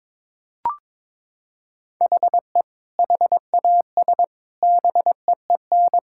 Morse code, heard as a keyed tone, about 22 wpm, 700 Hz. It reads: HE HAS BEEN
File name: Sentences with 7-Word Phrases - 22wpm